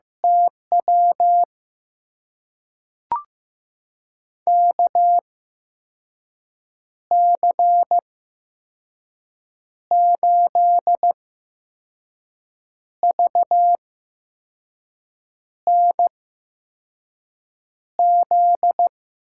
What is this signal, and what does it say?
Morse code, heard as a keyed tone, about 15 wpm, spaced as for 5 wpm, 700 Hz. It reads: TW KC8VNZ
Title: USA Call Signs - 6 Characters - 8x Character Spacing - 15wpm